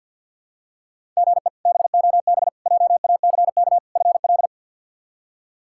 Morse code, heard as a keyed tone, about 52 wpm, 700 Hz. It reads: GE BOB JACK FL
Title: K1USN SST QSO Practice - 50wpm